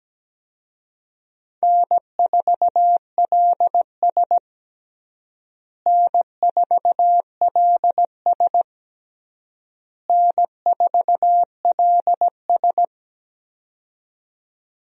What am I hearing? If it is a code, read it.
Morse code, heard as a keyed tone, about 17 wpm, 700 Hz. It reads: N4LS N4LS N4LS
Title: USA Call Signs - 4 Characters - Repeated 3x - 17wpm